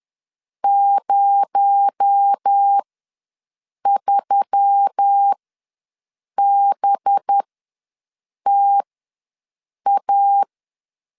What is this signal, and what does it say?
Morse code, heard as a keyed tone, about 10 wpm, spaced as for 6 wpm, 800 Hz. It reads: 03BTA
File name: LEZIONI / 7 GF9Z / 7 GF9 ws 10wpm 4034